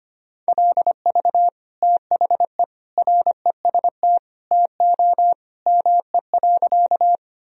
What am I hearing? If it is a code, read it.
Morse code, heard as a keyed tone, about 25 wpm, 700 Hz. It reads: LV THE REST TO ME.
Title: Sentences from Top 500 Words and Abbreviations 25wpm